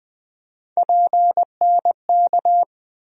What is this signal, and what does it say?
Morse code, heard as a keyed tone, about 20 wpm, 700 Hz. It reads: PNK
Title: USA Call Sign Postfixes - 3 Letters 20wpm